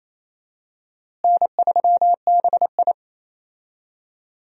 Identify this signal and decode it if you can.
Morse code, heard as a keyed tone, about 28 wpm, 700 Hz. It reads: N3BI
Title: USA Call Signs - 4 Characters - Repeated 2x - 28wpm